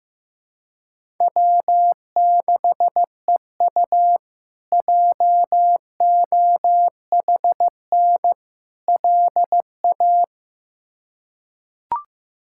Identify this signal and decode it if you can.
Morse code, heard as a keyed tone, about 15 wpm, 700 Hz. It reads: W6EU JOHN LA
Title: K1USN SST QSO Practice - 15wpm